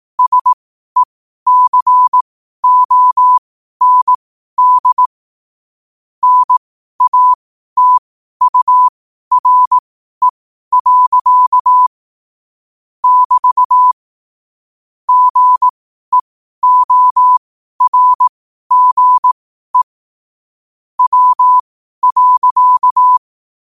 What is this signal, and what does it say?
Morse code, heard as a keyed tone, about 18 wpm, spaced as for 12 wpm, 1000 Hz. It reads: SECOND NATURE. <BT> GEORGE W.
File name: qotdfiles / 12wpm